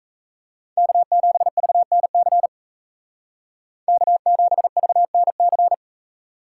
Morse code, heard as a keyed tone, about 37 wpm, 700 Hz. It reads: K7VNC K7VNC